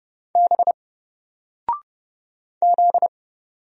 Morse code, heard as a keyed tone, about 29 wpm, 700 Hz. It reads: B Z
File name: Single-Letter-Number-30wpm